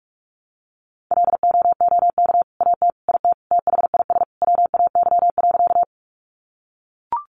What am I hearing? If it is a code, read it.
Morse code, heard as a keyed tone, about 44 wpm, 700 Hz. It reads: LOOK AT IT THIS WAY.
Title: Sentences from Top 100 Words 45wpm